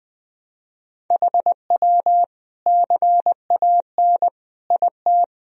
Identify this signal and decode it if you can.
Morse code, heard as a keyed tone, about 20 wpm, 700 Hz. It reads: HW CAN IT
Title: Sentences from Top 100 Words and Abbreviations 20wpm